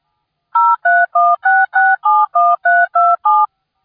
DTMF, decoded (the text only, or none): 0316671327